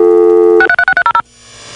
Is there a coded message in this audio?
Yes: DTMF from 0.6 to 1.2 s, heard as telephone touch tones, tones of about 70 ms. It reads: 63630